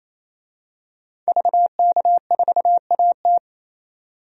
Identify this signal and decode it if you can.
Morse code, heard as a keyed tone, about 28 wpm, 700 Hz. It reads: VK4AT
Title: DXCC-100 Call Signs - 5 Characters - 28wpm